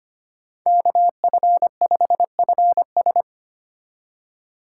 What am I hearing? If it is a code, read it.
Morse code, heard as a keyed tone, about 25 wpm, 700 Hz. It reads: KF5FS
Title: USA Call Signs - 5 Characters - 2x Character Spacing - 25wpm